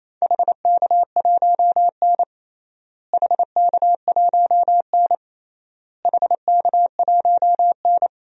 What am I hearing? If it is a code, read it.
Morse code, heard as a keyed tone, about 28 wpm, 700 Hz. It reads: HK1N HK1N HK1N
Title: DXCC-100 Call Signs - 4 Characters - Repeated 3x - 28wpm